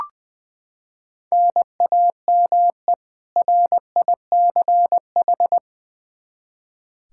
Morse code, heard as a keyed tone, about 20 wpm, 700 Hz. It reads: NAME RICH
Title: K1USN SST QSO Practice 20wpm